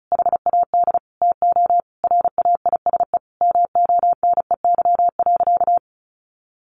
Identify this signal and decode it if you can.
Morse code, heard as a keyed tone, about 34 wpm, 700 Hz. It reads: HAD TO RAISE MONEY.
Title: Sentences from Top 1400 Words 35wpm